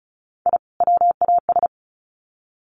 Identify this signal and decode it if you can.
Morse code, heard as a keyed tone, about 34 wpm, 700 Hz. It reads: I WAS
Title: Sets of 2 Words 35wpm